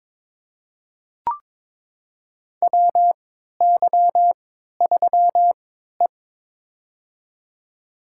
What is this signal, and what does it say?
Morse code, heard as a keyed tone, about 22 wpm, spaced as for 13 wpm, 700 Hz. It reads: WY3E
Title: USA Call Signs - 4 Characters - 3x Character Spacing - 22wpm